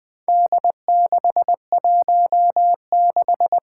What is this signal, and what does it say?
Morse code, heard as a keyed tone, about 20 wpm, 700 Hz. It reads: D616
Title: Four Challenging Letter-Number Combinations 20wpm